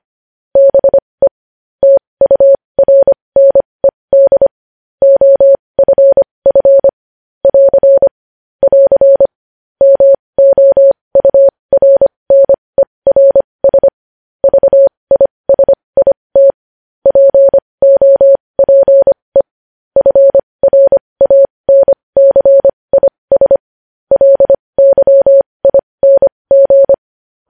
Morse code, heard as a keyed tone, about 25 wpm, spaced as for 20 wpm, 550 Hz. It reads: BE TURNED OFF <AR> <AR> MOURNERS VISIT POPE FRANCIS LYING
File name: podcast / News headlines-1x-25x20